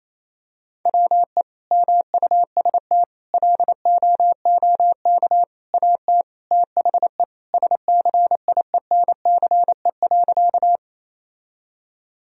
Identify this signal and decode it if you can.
Morse code, heard as a keyed tone, about 28 wpm, 700 Hz. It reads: WE MUST LOOK AT THE SCIENCE.
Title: Sentences from Top 1000 Words - Review Part 2 - 28wpm